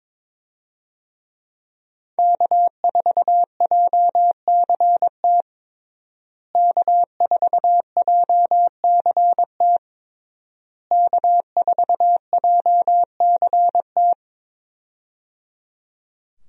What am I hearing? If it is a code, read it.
Morse code, heard as a keyed tone, about 22 wpm, 690 Hz. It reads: K4JCT K4JCT K4JCT
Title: USA Call Signs - 5 Characters - Repeated 3x - 22wpm